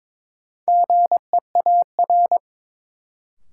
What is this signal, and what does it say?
Morse code, heard as a keyed tone, about 22 wpm, 700 Hz. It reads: GEAR
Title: Sentences from Top 1400 Words - New Word Drill - 22wpm